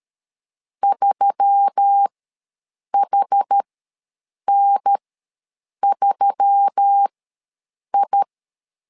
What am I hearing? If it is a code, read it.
Morse code, heard as a keyed tone, about 13 wpm, spaced as for 7 wpm, 800 Hz. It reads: 3HN3I
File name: LEZIONI / 4 Q1S3 ws 12wpm 2038